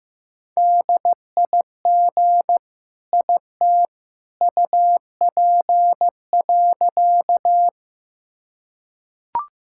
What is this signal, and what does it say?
Morse code, heard as a keyed tone, about 15 wpm, 700 Hz. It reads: DIG IT UP.
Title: Sentences from Top 1600 Words 15wpm